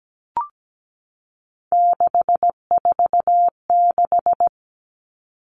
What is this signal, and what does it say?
Morse code, heard as a keyed tone, about 17 wpm, 700 Hz. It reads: 646
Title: Numbers - 3-Digit 17wpm